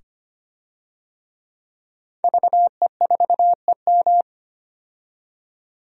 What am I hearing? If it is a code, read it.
Morse code, heard as a keyed tone, about 25 wpm, 700 Hz. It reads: VE4EM
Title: DXCC-100 Call Signs - 5 Characters - 25wpm